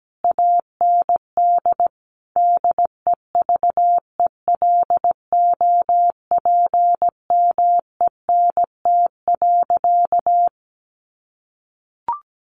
Morse code, heard as a keyed tone, about 17 wpm, 700 Hz. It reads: AND DEVELOPMENT.
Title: Sentences from Top 1400 Words 17wpm